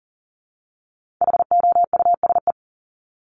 Morse code, heard as a keyed tone, about 40 wpm, 700 Hz. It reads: HOUSE
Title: Top 200 Words Repeated 40wpm